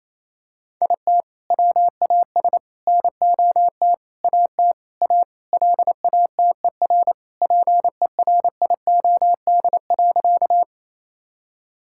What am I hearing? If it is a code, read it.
Morse code, heard as a keyed tone, about 28 wpm, 700 Hz. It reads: IT WAS NOT AT A LATER PERIOD.
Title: Sentences from Top 1000 Words - Review Part 5 - 28wpm